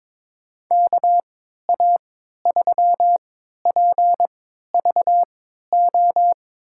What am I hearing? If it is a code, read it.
Morse code, heard as a keyed tone, about 22 wpm, spaced as for 12 wpm, 700 Hz. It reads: KA3PVO
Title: USA Call Signs - 6 Characters - 3x Character Spacing - 22wpm